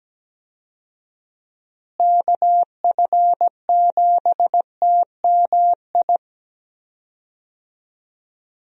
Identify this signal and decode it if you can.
Morse code, heard as a keyed tone, about 17 wpm, 700 Hz. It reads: KF7TMI
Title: USA Call Signs - 6 Characters - Repeated 3x - 17wpm